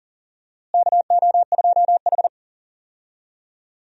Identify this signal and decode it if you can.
Morse code, heard as a keyed tone, about 40 wpm, 700 Hz. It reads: KO2H
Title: USA Call Signs - 4 Characters - Repeated 3x - 40wpm